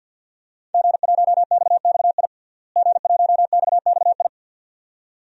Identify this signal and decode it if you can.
Morse code, heard as a keyed tone, about 47 wpm, 700 Hz. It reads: G1XXI G1XXI
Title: DXCC-100 Call Signs - 5 Characters - Repeated 2x - 50wpm